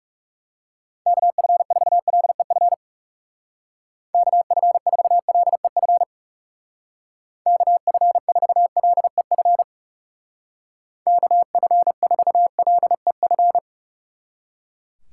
Morse code, heard as a keyed tone, about 35 wpm, 700 Hz. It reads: KF4LEF KF4LEF KF4LEF KF4LEF
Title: Speed Racing - USA 2x3 Call Signs 30wpm